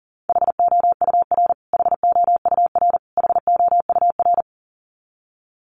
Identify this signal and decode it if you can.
Morse code, heard as a keyed tone, about 40 wpm, 700 Hz. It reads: HOUR HOUR HOUR
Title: Top 500 Words Repeated 40wpm